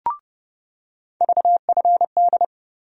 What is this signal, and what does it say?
Morse code, heard as a keyed tone, about 30 wpm, 700 Hz. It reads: VFD
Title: USA Call Sign Postfixes - 3 Letters 30wpm